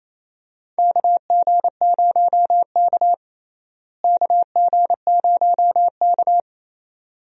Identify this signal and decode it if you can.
Morse code, heard as a keyed tone, about 28 wpm, 700 Hz. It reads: KG0K KG0K
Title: USA Call Signs - 4 Characters - Repeated 2x - 28wpm